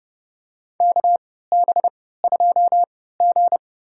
Morse code, heard as a keyed tone, about 30 wpm, spaced as for 17 wpm, 700 Hz. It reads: KB2G
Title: USA Call Signs - 3x Character Spacing - 30wpm